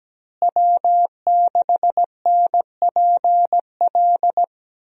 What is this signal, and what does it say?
Morse code, heard as a keyed tone, about 17 wpm, 700 Hz. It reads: W6NPL
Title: USA Call Signs - 5 Characters - Repeated 3x - 17wpm